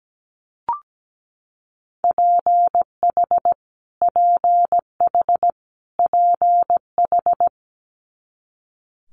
Morse code, heard as a keyed tone, about 17 wpm, 700 Hz. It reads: PH PH PH
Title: Multi-letter Phonemes Repeated 3x 17wpm